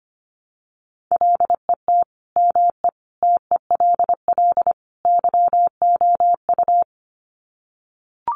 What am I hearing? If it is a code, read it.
Morse code, heard as a keyed tone, about 25 wpm, 700 Hz. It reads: LET ME TELL YOU
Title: Sets of 4 Words 25wpm